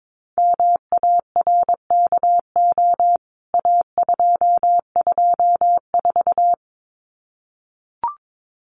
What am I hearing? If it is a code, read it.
Morse code, heard as a keyed tone, about 22 wpm, 700 Hz. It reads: MARKO A224
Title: CWT Exchange 22wpm